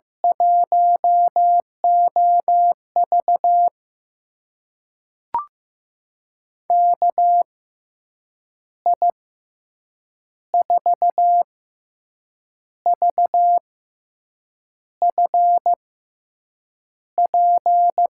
Morse code, heard as a keyed tone, about 15 wpm, spaced as for 6 wpm, 700 Hz. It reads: 1OV KI4VFP
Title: USA Call Signs - 6x Character Spacing - 15wpm